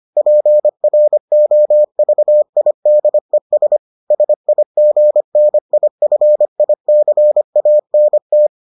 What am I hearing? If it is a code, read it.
Morse code, heard as a keyed tone, about 25 wpm, 600 Hz. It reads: PROVIDES SIGNIFICANT